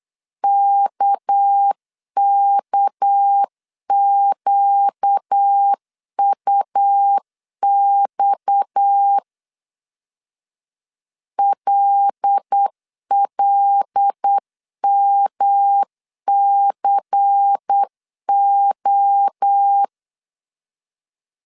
Morse code, heard as a keyed tone, about 8 wpm, 800 Hz. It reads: KKQUX LLMCO
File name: CW / DET / 1 6 1500C S C 8wpm 5219